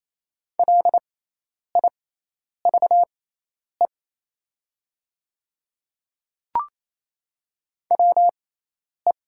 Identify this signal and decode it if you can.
Morse code, heard as a keyed tone, about 28 wpm, spaced as for 10 wpm, 700 Hz. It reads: LIVE WE
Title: 6x Character Spacing - Top 200 Words - 28wpm